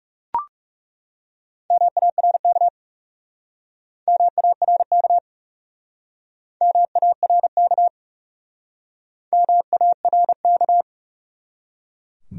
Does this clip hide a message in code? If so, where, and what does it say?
Yes: Morse code from 1.7 to 10.8 s, heard as a keyed tone, about 38 wpm, 700 Hz. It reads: MARK MARK MARK MARK